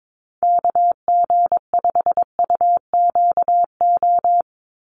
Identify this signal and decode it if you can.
Morse code, heard as a keyed tone, about 22 wpm, 700 Hz. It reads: KG5UQO
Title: USA Call Signs 22wpm